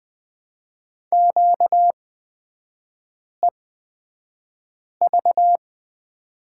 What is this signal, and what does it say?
Morse code, heard as a keyed tone, about 20 wpm, spaced as for 5 wpm, 700 Hz. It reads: QEV
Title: 236 Lesson 38e - Letter q - Three Characters 20wpm